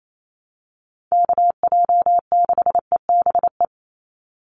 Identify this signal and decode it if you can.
Morse code, heard as a keyed tone, about 28 wpm, 700 Hz. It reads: KJ6EBE
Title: USA Call Signs 28wpm